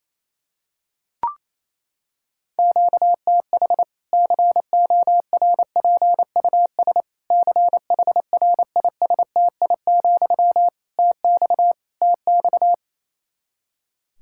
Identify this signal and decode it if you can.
Morse code, heard as a keyed tone, about 28 wpm, 700 Hz. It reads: QTH CORPUS CHRISTI, TX TX
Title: K1USN SST QSO Practice 28wpm